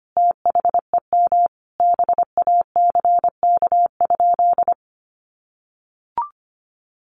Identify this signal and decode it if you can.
Morse code, heard as a keyed tone, about 25 wpm, 700 Hz. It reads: THEM BACK?
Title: Sentences from Top 1500 Words 25wpm